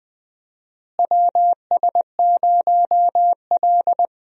Morse code, heard as a keyed tone, about 20 wpm, 700 Hz. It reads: WS0L
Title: USA Call Signs - 4 Characters - Repeated 3x - 20wpm